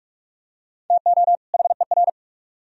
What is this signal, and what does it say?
Morse code, heard as a keyed tone, about 43 wpm, 700 Hz. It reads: TO HER